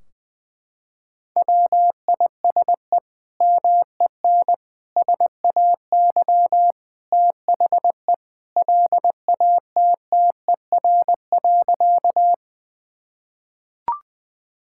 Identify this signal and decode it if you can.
Morse code, heard as a keyed tone, about 20 wpm, 700 Hz. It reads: WISE MEN SAY THE LATTER.
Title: Sentences from Top 1600 Words 20wpm